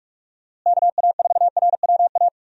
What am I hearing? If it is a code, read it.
Morse code, heard as a keyed tone, about 42 wpm, 700 Hz. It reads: KA4RWA
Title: USA Call Signs - 6 Characters - Repeated 2x - 45wpm